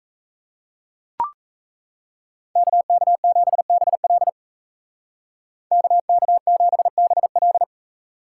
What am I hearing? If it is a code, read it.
Morse code, heard as a keyed tone, about 33 wpm, 700 Hz. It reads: KK7BL KK7BL
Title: Speed Racing - USA 2x2 Call Signs 28wpm